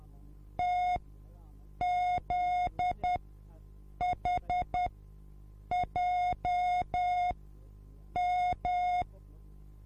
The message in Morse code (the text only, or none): TZHJM